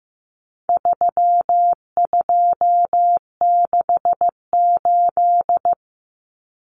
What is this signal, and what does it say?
Morse code, heard as a keyed tone, about 15 wpm, 700 Hz. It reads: 3268
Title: Numbers - 4-Digit 15wpm